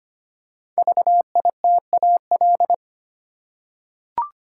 Morse code, heard as a keyed tone, about 25 wpm, 700 Hz. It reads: VITAL
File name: 5 Letter Words - First 3 Letters Given - 25wpm